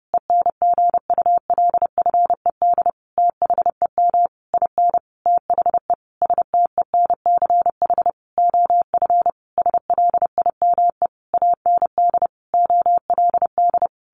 Morse code, heard as a keyed tone, about 30 wpm, 700 Hz. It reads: ENGULFED THEM IN THE STENCH OF SLIME AND OLD
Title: A Sound Of Thunder 30wpm